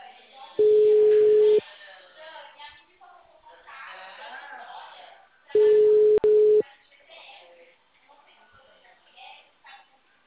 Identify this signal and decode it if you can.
Ringback tone (Europe, 425 Hz), with speech.